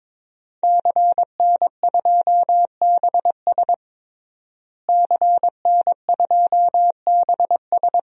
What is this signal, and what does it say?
Morse code, heard as a keyed tone, about 22 wpm, 700 Hz. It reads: CN2BS CN2BS
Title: DXCC-100 Call Signs - 5 Characters - Repeated 2x - 22wpm